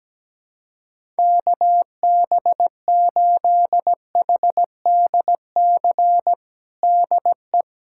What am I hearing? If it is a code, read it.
Morse code, heard as a keyed tone, about 17 wpm, 700 Hz. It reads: KB8HDC DE
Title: USA Callsign DE Callsign - 6 Characters - 17wpm